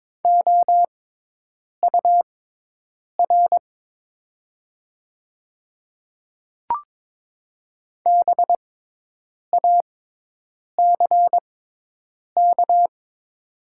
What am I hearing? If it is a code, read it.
Morse code, heard as a keyed tone, about 22 wpm, spaced as for 8 wpm, 700 Hz. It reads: OUR BACK
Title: 6x Character Spacing - Top 200 Words - 22wpm